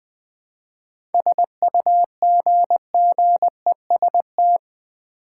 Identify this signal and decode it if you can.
Morse code, heard as a keyed tone, about 20 wpm, 700 Hz. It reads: SUGGEST
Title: Seven Letter Words - 20wpm